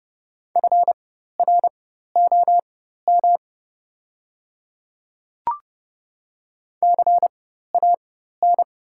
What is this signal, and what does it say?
Morse code, heard as a keyed tone, about 30 wpm, spaced as for 14 wpm, 700 Hz. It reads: FROM CAN
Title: 4x Character Spacing - Top 100 Words - 30wpm